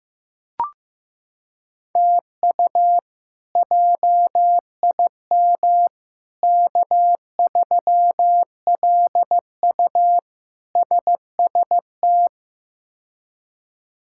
Morse code, heard as a keyed tone, about 15 wpm, 700 Hz. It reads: TU JIM K3LU SST
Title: K1USN SST QSO Practice - 15wpm